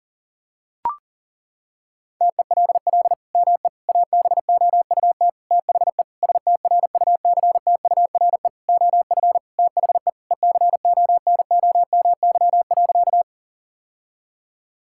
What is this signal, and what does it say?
Morse code, heard as a keyed tone, about 37 wpm, 700 Hz. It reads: TELL ME ABOUT THE STRUCTURE OF THE ECONOMY.
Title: Sentences from Top 1000 Words - Review Part 6 - 40wpm